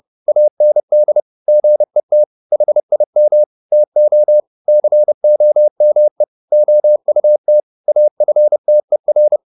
Morse code, heard as a keyed tone, about 30 wpm, 600 Hz. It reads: AND GET HIM TO COME OUT AFTER
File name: the old man and the sea / the old man and the sea 1 30wpm